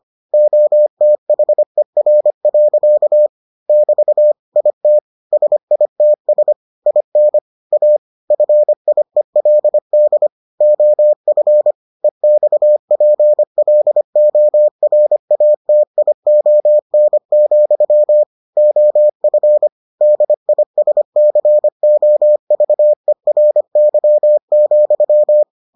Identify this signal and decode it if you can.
Morse code, heard as a keyed tone, about 25 wpm, 600 Hz. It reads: OTHER. <BT> IT SITS IN A FIELD OF EXPLORATION, OF DISCOVERY,